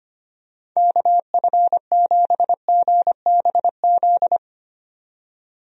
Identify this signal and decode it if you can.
Morse code, heard as a keyed tone, about 25 wpm, 700 Hz. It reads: KF7GBZ
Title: USA Call Signs - 6 Characters - 25wpm